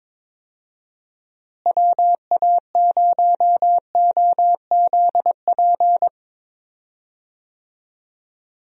Morse code, heard as a keyed tone, about 22 wpm, 700 Hz. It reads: WA0OZP
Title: USA Call Signs - 6 Characters - Repeated 2x - 22wpm